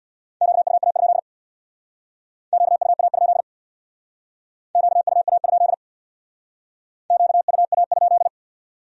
Morse code, heard as a keyed tone, about 59 wpm, 700 Hz. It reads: QUA? QUA? QUA? QUA?
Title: Speed-Racing Q-Codes - 50wpm